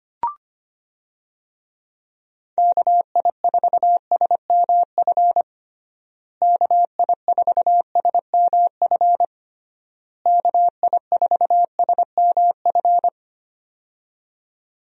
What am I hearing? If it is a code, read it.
Morse code, heard as a keyed tone, about 25 wpm, 700 Hz. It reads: KI4SMF KI4SMF KI4SMF
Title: USA Call Signs - 6 Characters - Repeated 3x - 25wpm